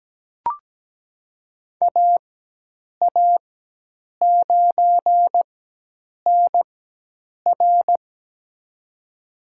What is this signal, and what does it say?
Morse code, heard as a keyed tone, about 17 wpm, spaced as for 8 wpm, 700 Hz. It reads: AA9NR
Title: USA Call Signs - 5 Characters - 4x Character Spacing - 17wpm